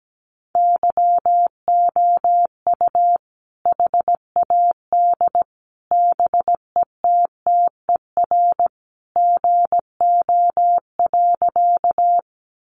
Morse code, heard as a keyed tone, about 17 wpm, 700 Hz. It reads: YOU HAD BETTER GO.
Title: Sentences from Top 400 Words 17wpm